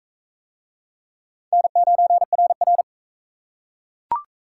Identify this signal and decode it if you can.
Morse code, heard as a keyed tone, about 42 wpm, 700 Hz. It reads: N9RR